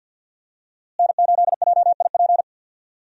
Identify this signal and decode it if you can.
Morse code, heard as a keyed tone, about 48 wpm, 700 Hz. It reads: N8JIP
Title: USA Call Signs - 5 Characters - Repeated 2x - 50wpm